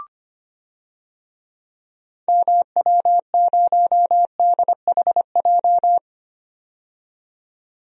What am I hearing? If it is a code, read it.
Morse code, heard as a keyed tone, about 25 wpm, 700 Hz. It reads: MW0DHJ
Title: DXCC-100 Call Signs - 6 Characters - 25wpm